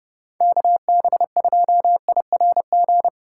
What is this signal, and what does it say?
Morse code, heard as a keyed tone, about 37 wpm, 700 Hz. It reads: KB2IRG
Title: Speed Racing - USA 2x3 Call Signs 30wpm